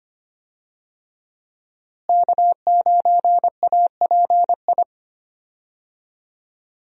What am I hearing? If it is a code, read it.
Morse code, heard as a keyed tone, about 25 wpm, 700 Hz. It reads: K9API
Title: USA Call Signs - 5 Characters - Repeated 3x - 25wpm